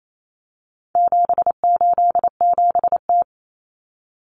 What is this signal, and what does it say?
Morse code, heard as a keyed tone, about 28 wpm, 700 Hz. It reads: 787T
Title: Common Cut Numbers - 4-Digit 28wpm